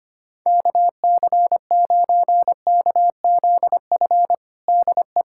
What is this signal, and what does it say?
Morse code, heard as a keyed tone, about 25 wpm, 700 Hz. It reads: KC9KZF DE
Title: USA Callsign DE Callsign - 6 Characters - 25wpm